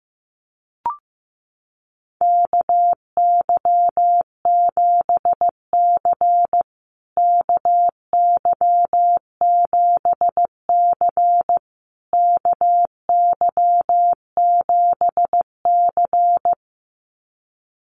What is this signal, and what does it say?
Morse code, heard as a keyed tone, about 14 wpm, 700 Hz. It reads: KY7C KY7C KY7C
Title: USA Call Signs Repeated 3x 15wpm